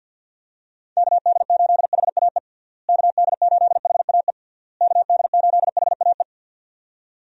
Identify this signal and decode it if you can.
Morse code, heard as a keyed tone, about 51 wpm, 700 Hz. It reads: KD8HAE KD8HAE KD8HAE
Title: USA Call Signs - 6 Characters - Repeated 3x - 50wpm